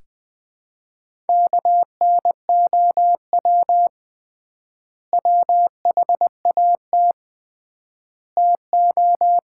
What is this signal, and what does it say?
Morse code, heard as a keyed tone, about 21 wpm, 700 Hz. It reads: KNOW WHAT TO
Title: 3x Word-Spacing - Sets of 5 Words - 20wpm